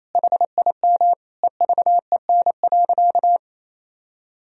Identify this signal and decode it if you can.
Morse code, heard as a keyed tone, about 28 wpm, 700 Hz. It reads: HIM EVEN.
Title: Sentences from Top 1000 Words - Review Part 1 - 28wpm